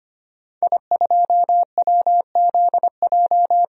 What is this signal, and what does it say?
Morse code, heard as a keyed tone, about 25 wpm, 700 Hz. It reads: I2WZJ
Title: DXCC-100 Call Signs - 5 Characters - 25wpm